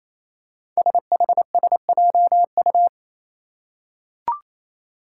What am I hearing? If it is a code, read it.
Morse code, heard as a keyed tone, about 28 wpm, 700 Hz. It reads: SHSJU
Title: Five Challenging Letter-Number Combinations 28wpm